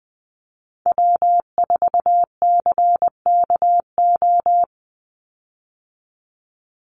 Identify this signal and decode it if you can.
Morse code, heard as a keyed tone, about 19 wpm, 700 Hz. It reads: W4CKO
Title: USA Call Signs 20wpm